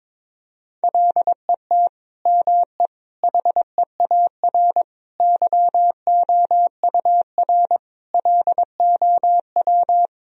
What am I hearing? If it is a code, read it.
Morse code, heard as a keyed tone, about 22 wpm, 700 Hz. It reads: LET ME HEAR YOUR LOW
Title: Sentences from Top 1000 Words - Review Part 1 - 22wpm